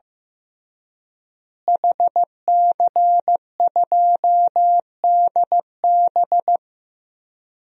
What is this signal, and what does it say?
Morse code, heard as a keyed tone, about 15 wpm, 700 Hz. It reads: HC2DB
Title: DXCC-100 Call Signs - 5 Characters - Repeated 2x - 15wpm